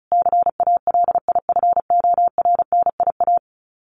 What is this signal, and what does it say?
Morse code, heard as a keyed tone, about 34 wpm, 700 Hz. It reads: CALIFORNIA
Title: Sets of 5 Words Encore2 35wpm